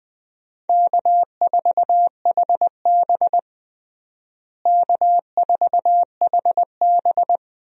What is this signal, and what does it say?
Morse code, heard as a keyed tone, about 20 wpm, 700 Hz. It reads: K4HB K4HB
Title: USA Call Signs - 4 Characters - Repeated 3x - 20wpm